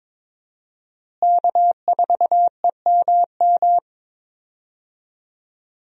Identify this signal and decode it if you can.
Morse code, heard as a keyed tone, about 22 wpm, 700 Hz. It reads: K4EMM
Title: USA Call Signs - 5 Characters - Repeated 2x - 22wpm